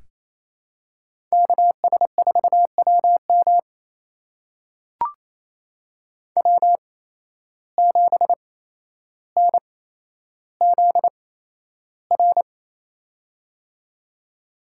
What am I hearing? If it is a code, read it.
Morse code, heard as a keyed tone, about 28 wpm, spaced as for 12 wpm, 700 Hz. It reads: KS4WM W7NZR
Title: USA Call Signs - 8x Character Spacing - 28wpm